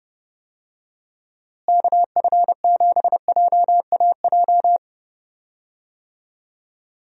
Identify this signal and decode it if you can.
Morse code, heard as a keyed tone, about 30 wpm, 700 Hz. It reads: KF7JAJ
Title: USA Call Signs - 6 Characters - Repeated 2x - 30wpm